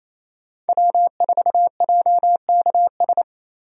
Morse code, heard as a keyed tone, about 27 wpm, 700 Hz. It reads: W4JKS
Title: USA Call Signs - 5 Characters - 28wpm